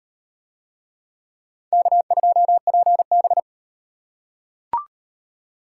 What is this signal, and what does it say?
Morse code, heard as a keyed tone, about 35 wpm, 700 Hz. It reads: K2PB